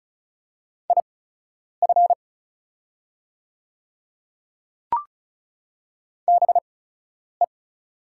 Morse code, heard as a keyed tone, about 36 wpm, spaced as for 10 wpm, 700 Hz. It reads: IF BE
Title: 8x Character Spacing - Top 100 Words - 35wpm